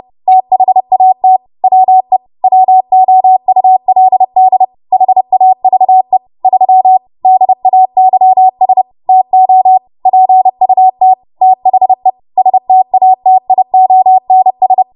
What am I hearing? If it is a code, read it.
Morse code, heard as a keyed tone, about 30 wpm, 750 Hz. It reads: THAT WE WOULD HAVE 3 DAYS TO PUT THE STATIONS
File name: Morse / 30WPM